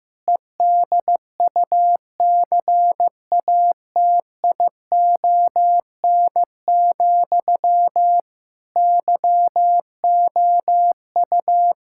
Morse code, heard as a keyed tone, about 14 wpm, 700 Hz. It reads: EDUCATION, YOU
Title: Sentences from Top 1000 Words - Review Part 6 - 15wpm